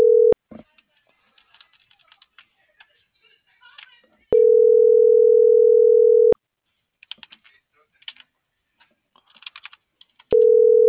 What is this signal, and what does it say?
Ringback tone (North America), 440 + 480 Hz.
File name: CENTRAL MABE / out luis.palacios - .79105